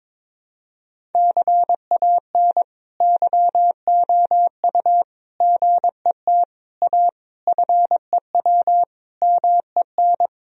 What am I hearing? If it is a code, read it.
Morse code, heard as a keyed tone, about 22 wpm, 700 Hz. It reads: CAN YOU GET A FEW MEN
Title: Sentences from Top 1000 Words - Review Part 1 - 22wpm